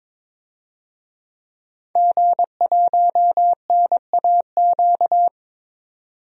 Morse code, heard as a keyed tone, about 22 wpm, 700 Hz. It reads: G1NAQ
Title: DXCC-100 Call Signs - 5 Characters - 22wpm